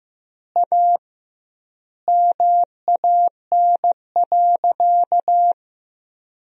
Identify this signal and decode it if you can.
Morse code, heard as a keyed tone, about 15 wpm, 700 Hz. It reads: A MAN.
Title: 2x Word-Spacing - Sentences from Top 300 Words - 15wpm